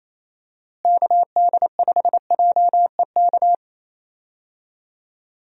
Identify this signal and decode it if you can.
Morse code, heard as a keyed tone, about 28 wpm, 700 Hz. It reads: KD5JEK
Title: USA Call Signs - 6 Characters - Repeated 3x - 28wpm